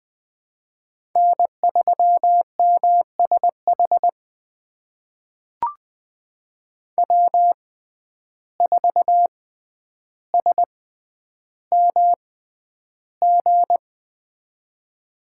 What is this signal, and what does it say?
Morse code, heard as a keyed tone, about 20 wpm, spaced as for 10 wpm, 700 Hz. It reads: N3MSH W4SMG
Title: USA Call Signs - 5 Characters - 6x Character Spacing - 20wpm